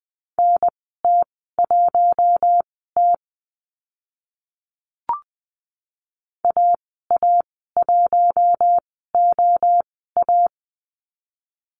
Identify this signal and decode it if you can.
Morse code, heard as a keyed tone, about 20 wpm, spaced as for 15 wpm, 700 Hz. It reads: NT1T AA1OA
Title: 042 Lesson 8g - Number 1 - Call Signs 20wpm